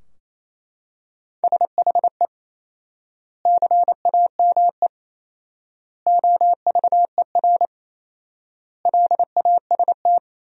Morse code, heard as a keyed tone, about 27 wpm, 700 Hz. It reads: SHE CAME OVER LAST
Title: 4x Word-Spacing - Sentences from Top 300 Words - 28wpm